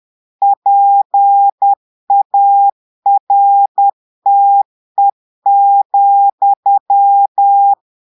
Morse code, heard as a keyed tone, about 10 wpm, 800 Hz. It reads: PARTE,
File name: Portugais - texte en clair 1 - vitesse - 10 mots minute